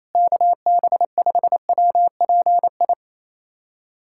Morse code, heard as a keyed tone, about 27 wpm, 700 Hz. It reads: KB5WPI
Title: USA Callsign DE Callsign - 6 Characters - 28wpm